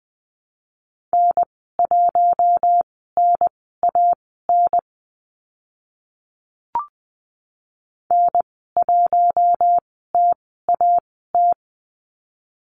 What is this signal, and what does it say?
Morse code, heard as a keyed tone, about 20 wpm, spaced as for 14 wpm, 700 Hz. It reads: N1NAN N1TAT
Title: 042 Lesson 8g - Number 1 - Call Signs 20wpm